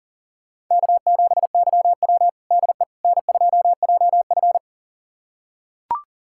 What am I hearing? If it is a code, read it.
Morse code, heard as a keyed tone, about 40 wpm, 700 Hz. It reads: K7YW DE N2JF
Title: USA Callsign DE Callsign - 4 Characters - 40wpm